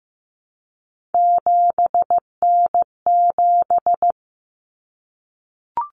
Morse code, heard as a keyed tone, about 15 wpm, 700 Hz. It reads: 7N7
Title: Common Cut Numbers - 3-Digit 15wpm